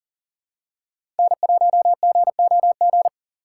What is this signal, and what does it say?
Morse code, heard as a keyed tone, about 40 wpm, 700 Hz. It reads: N1GOG
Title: USA Call Signs - 2x Character Spacing - 40wpm